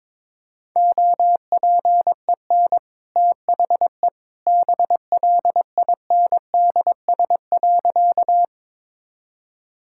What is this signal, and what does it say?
Morse code, heard as a keyed tone, about 22 wpm, 690 Hz. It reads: OPEN THE BLINDS.
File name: practice / 22wpm / Easy Sentences with Hints - 22wpm